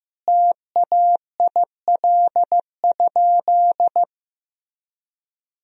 Morse code, heard as a keyed tone, about 15 wpm, 700 Hz. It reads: TAIL?
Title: Sentences from Top 1000 Words - Review Part 3 - 15wpm